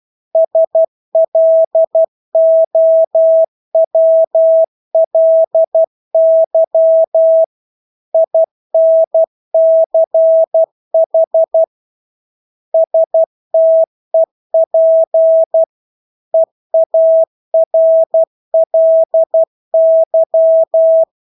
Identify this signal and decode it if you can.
Morse code, heard as a keyed tone, about 12 wpm, 650 Hz. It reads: SLOWLY INCH STEP EARLY
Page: Engelske ord 12 wpm